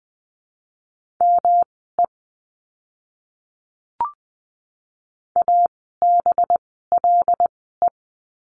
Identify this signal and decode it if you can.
Morse code, heard as a keyed tone, about 20 wpm, spaced as for 15 wpm, 700 Hz. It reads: ME ABLE
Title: 185 Lesson 30f - Letter b - Words 20wpm